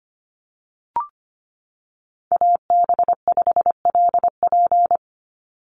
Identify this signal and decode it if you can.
Morse code, heard as a keyed tone, about 25 wpm, 700 Hz. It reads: AB5LP
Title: USA Call Signs 25wpm